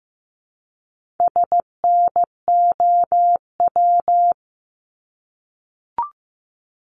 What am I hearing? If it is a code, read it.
Morse code, heard as a keyed tone, about 15 wpm, 700 Hz. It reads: SNOW
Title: Multi-letter Phonemes Repeated 3x 15wpm